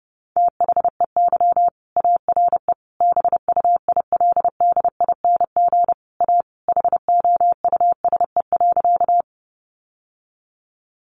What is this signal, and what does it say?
Morse code, heard as a keyed tone, about 30 wpm, 700 Hz. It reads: THEY ARE BUILDING A HOUSE.
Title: Sentences from Top 700 Words 30wpm